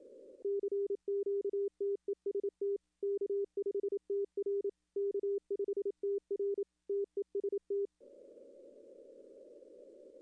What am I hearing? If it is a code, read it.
Morse code, heard as a keyed tone, about 28 wpm, 400 Hz. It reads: CQTEST K5TR K5TR TEST
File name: contests / arrl-160-k5tr-op-w5tn -2300